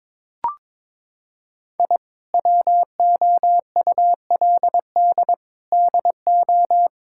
Morse code, heard as a keyed tone, about 22 wpm, 700 Hz. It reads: I WOULD DO
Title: Sentences from Top 1000 Words - Review Part 6 - 22wpm